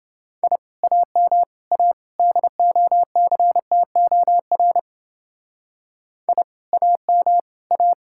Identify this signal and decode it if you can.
Morse code, heard as a keyed tone, about 28 wpm, 700 Hz. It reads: I AM A DOCTOR I AM A